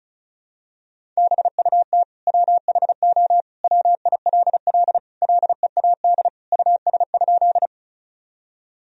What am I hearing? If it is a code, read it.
Morse code, heard as a keyed tone, about 34 wpm, 700 Hz. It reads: BUT WHO WILL LEAD US?
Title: Sentences from Top 1000 Words - Review Part 2 - 35wpm